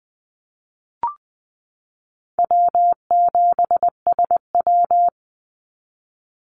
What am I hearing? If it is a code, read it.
Morse code, heard as a keyed tone, about 20 wpm, 700 Hz. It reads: W7SW
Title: Easy Call Signs Repeated 2x - 20wpm